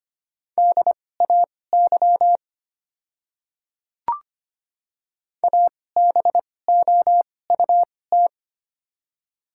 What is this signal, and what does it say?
Morse code, heard as a keyed tone, about 25 wpm, spaced as for 18 wpm, 700 Hz. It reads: DAY ABOUT
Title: 2x Character Spacing - Top 500 Words - 25wpm